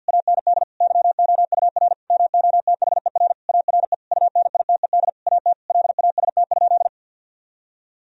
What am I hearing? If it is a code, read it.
Morse code, heard as a keyed tone, about 50 wpm, 700 Hz. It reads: AND YOUR MOTHER ARE UNITED AT LAST?
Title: Sentences from Top 1000 Words - Review Part 4 - 50wpm